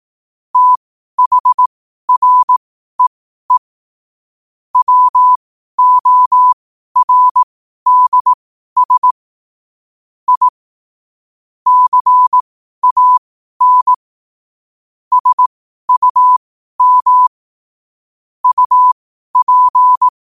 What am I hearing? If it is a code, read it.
Morse code, heard as a keyed tone, about 18 wpm, spaced as for 12 wpm, 1000 Hz. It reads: THREE WORDS I CAN SUM UP
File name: qotdfiles / 12wpm